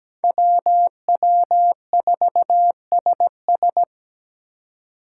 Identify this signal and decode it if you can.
Morse code, heard as a keyed tone, about 18 wpm, 700 Hz. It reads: WW4SS
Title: Speed Racing - USA 2x2 Call Signs 17wpm